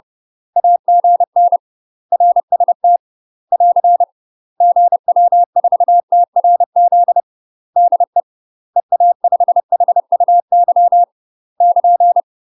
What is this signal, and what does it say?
Morse code, heard as a keyed tone, about 30 wpm, 700 Hz. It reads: AGN RST <AR> GW4TRZ DE EA5HUY <KN>
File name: UK MASTER QSOs TRAINING / 30WPM / UK MASTER QSOs 0008